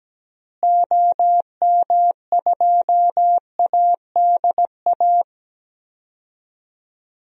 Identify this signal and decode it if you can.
Morse code, heard as a keyed tone, about 17 wpm, 700 Hz. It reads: OM2ADA
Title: DXCC-100 Call Signs - 6 Characters - Repeated 3x - 17wpm